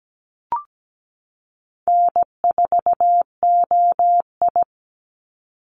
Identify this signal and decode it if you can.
Morse code, heard as a keyed tone, about 17 wpm, 700 Hz. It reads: N4OI
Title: Easy Call Signs Repeated 2x - 17wpm